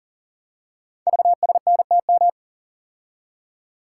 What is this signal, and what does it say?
Morse code, heard as a keyed tone, about 43 wpm, 690 Hz. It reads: VSNTM
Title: Code Groups - 5 Letter:Numbers - 40wpm